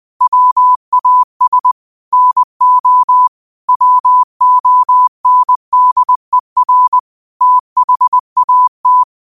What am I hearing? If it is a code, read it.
Morse code, heard as a keyed tone, about 20 wpm, 1000 Hz. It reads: WAS NO WONDER THAT
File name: qotdfiles / 20wpm